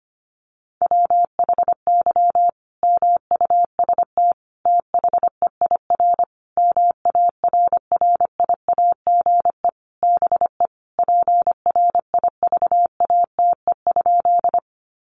Morse code, heard as a keyed tone, about 25 wpm, 700 Hz. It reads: WHY MUST THEIR MARRIAGE BE PRIVATE?
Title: Sentences from Top 1000 Words 25wpm